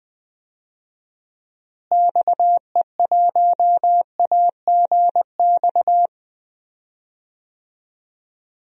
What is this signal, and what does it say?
Morse code, heard as a keyed tone, about 20 wpm, 700 Hz. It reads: XE1AGX
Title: DXCC-100 Call Signs - 6 Characters - Repeated 2x - 20wpm